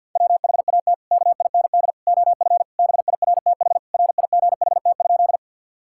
Morse code, heard as a keyed tone, about 50 wpm, 700 Hz. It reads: WHAT KIND OF BIRTH RIGHT?
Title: Sentences from Top 1000 Words - Review Part 5 - 50wpm